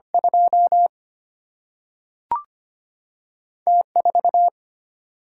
Morse code, heard as a keyed tone, about 25 wpm, 700 Hz. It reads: 2 T 4
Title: Single-Letter Single-Number 25wpm